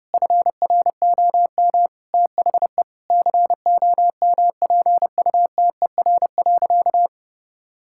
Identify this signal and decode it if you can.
Morse code, heard as a keyed tone, about 30 wpm, 700 Hz. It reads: FROM THE COMPUTER.
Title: Sentences from Top 1000 Words - Review Part 6 - 30wpm